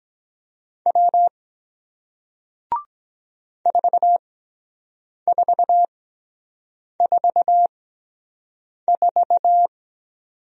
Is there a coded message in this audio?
Yes: Morse code from 0.9 to 9.7 s, heard as a keyed tone, about 23 wpm, spaced as for 7 wpm, 700 Hz. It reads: W 4444